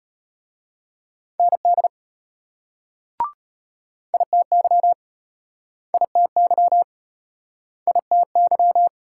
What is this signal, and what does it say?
Morse code, heard as a keyed tone, about 33 wpm, 700 Hz. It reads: ND ITY ITY ITY